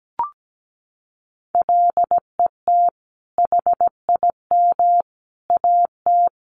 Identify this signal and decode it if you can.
Morse code, heard as a keyed tone, about 17 wpm, 700 Hz. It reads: LET HIM AT
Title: Sentences from Top 1500 Words 17wpm